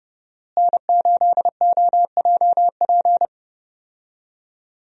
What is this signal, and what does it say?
Morse code, heard as a keyed tone, about 30 wpm, 700 Hz. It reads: N8OJP
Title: USA Call Signs - 5 Characters - Repeated 3x - 30wpm